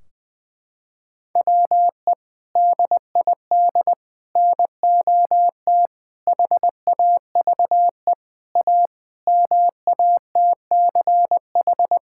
Morse code, heard as a keyed tone, about 20 wpm, 700 Hz. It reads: WE DID NOT HAVE A MATCH
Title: Sentences from Top 1600 Words 20wpm